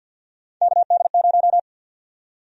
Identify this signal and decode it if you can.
Morse code, heard as a keyed tone, about 52 wpm, 700 Hz. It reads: KD0
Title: USA Call Sign Prefixes - 50wpm